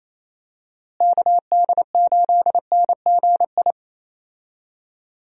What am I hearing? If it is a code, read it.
Morse code, heard as a keyed tone, about 28 wpm, 700 Hz. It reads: KD8NGI
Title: USA Call Signs - 6 Characters - Repeated 2x - 28wpm